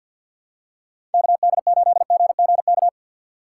Morse code, heard as a keyed tone, about 53 wpm, 700 Hz. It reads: KD8GGK
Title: USA Call Signs - 6 Characters - 8x Character Spacing - 50wpm